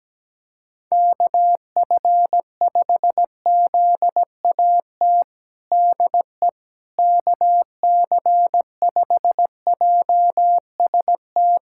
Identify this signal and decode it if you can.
Morse code, heard as a keyed tone, about 17 wpm, 700 Hz. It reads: KF5ZAT DE KC5JST
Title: USA Callsign DE Callsign - 6 Characters - 17wpm